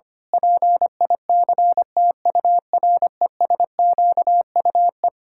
Morse code, heard as a keyed tone, about 25 wpm, 700 Hz. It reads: PICTURESQUE
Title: Pangram Challenge v2 - 25wpm